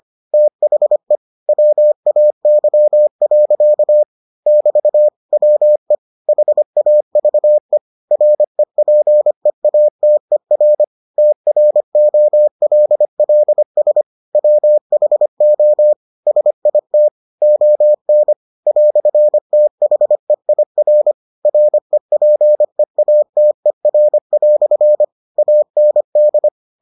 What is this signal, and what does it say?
Morse code, heard as a keyed tone, about 25 wpm, 600 Hz. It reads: THE WAY. <BT> WE HAVE REPEATER TROLLS WHO SIT ON "THEIR REPEATER" AND